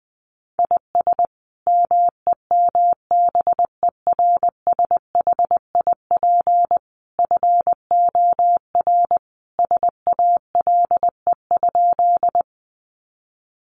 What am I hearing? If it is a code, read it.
Morse code, heard as a keyed tone, about 19 wpm, 700 Hz. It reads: IS MEMBERSHIP FOR SALE?
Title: Sentences from Top 1400 Words 20wpm